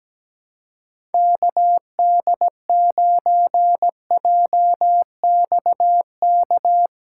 Morse code, heard as a keyed tone, about 17 wpm, 700 Hz. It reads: KD9JXK
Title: USA Callsign DE Callsign - 6 Characters - 17wpm